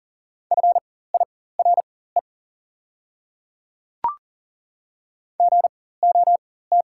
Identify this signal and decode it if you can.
Morse code, heard as a keyed tone, about 40 wpm, spaced as for 19 wpm, 700 Hz. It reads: FIRE GOT
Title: 4x Character Spacing - Top 500 Words - 40wpm